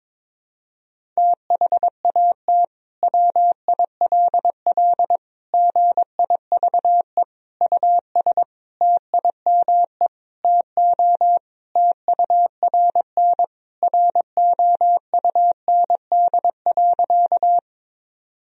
Morse code, heard as a keyed tone, about 22 wpm, 700 Hz. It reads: THAT WILL GIVE US TIME TO TURN ROUND.
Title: Sentences from Top 1000 Words - Review Part 1 - 22wpm